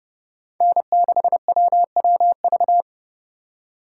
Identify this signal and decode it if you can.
Morse code, heard as a keyed tone, about 30 wpm, 700 Hz. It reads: N6WWV
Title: USA Call Signs - 5 Characters - 4x Character Spacing - 30wpm